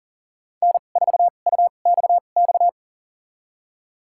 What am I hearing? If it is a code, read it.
Morse code, heard as a keyed tone, about 40 wpm, spaced as for 29 wpm, 700 Hz. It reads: N4UXX
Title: USA Call Signs - 2x Character Spacing - 40wpm